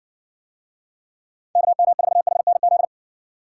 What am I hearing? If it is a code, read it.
Morse code, heard as a keyed tone, about 55 wpm, 700 Hz. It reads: KM4HNZ